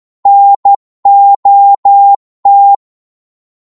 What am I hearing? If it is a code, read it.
Morse code, heard as a keyed tone, about 12 wpm, 800 Hz. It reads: NOT
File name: Morse / psg-12wpm